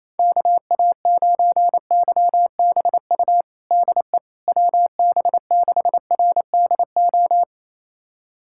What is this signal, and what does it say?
Morse code, heard as a keyed tone, about 28 wpm, 700 Hz. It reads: KA9YBU DE WB6RDO
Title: USA Callsign DE Callsign 28wpm